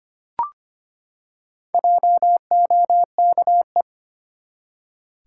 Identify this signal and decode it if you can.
Morse code, heard as a keyed tone, about 25 wpm, 700 Hz. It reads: JOKE
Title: Four Letter Words - 25wpm